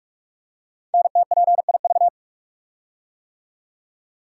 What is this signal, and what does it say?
Morse code, heard as a keyed tone, about 44 wpm, 700 Hz. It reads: NTPIV
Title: Code Groups - 5 Letter:Numbers - 45wpm